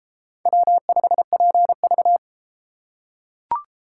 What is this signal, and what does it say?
Morse code, heard as a keyed tone, about 30 wpm, 700 Hz. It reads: W5PV